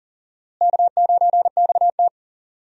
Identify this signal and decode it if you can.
Morse code, heard as a keyed tone, about 40 wpm, 700 Hz. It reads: K9XT
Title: USA Call Signs - 4 Characters - Repeated 2x - 40wpm